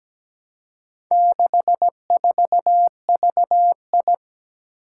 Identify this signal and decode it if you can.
Morse code, heard as a keyed tone, about 17 wpm, 700 Hz. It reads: 64VI
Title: Four Challenging Letter-Number Combinations 17wpm